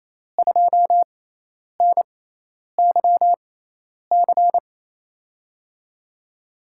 Morse code, heard as a keyed tone, about 28 wpm, spaced as for 10 wpm, 700 Hz. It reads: 2NYC
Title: USA Call Signs - 6 Characters - 6x Character Spacing - 28wpm